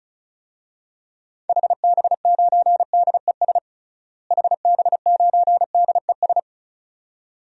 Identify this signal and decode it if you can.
Morse code, heard as a keyed tone, about 34 wpm, 700 Hz. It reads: HB9DES HB9DES
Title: DXCC-100 Call Signs - 6 Characters - Repeated 2x - 35wpm